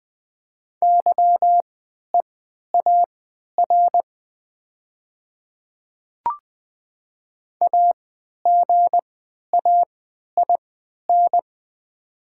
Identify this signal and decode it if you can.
Morse code, heard as a keyed tone, about 20 wpm, spaced as for 11 wpm, 700 Hz. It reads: YEAR AGAIN
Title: 3x Character Spacing - Top 200 Words - 20wpm